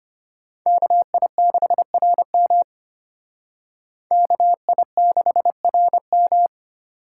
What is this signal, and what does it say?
Morse code, heard as a keyed tone, about 32 wpm, 700 Hz. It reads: KI6RM KI6RM